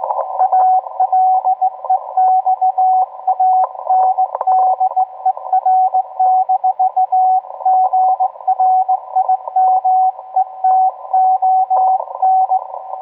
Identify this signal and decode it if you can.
Morse code, heard as a keyed tone, about 18 wpm, 750 Hz, with radio noise. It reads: ALEXANDER-GRIMETON